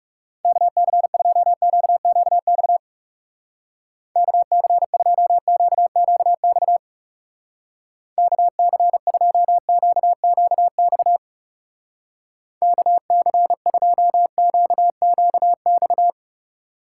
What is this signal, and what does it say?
Morse code, heard as a keyed tone, about 39 wpm, 700 Hz. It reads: KC2QQX KC2QQX KC2QQX KC2QQX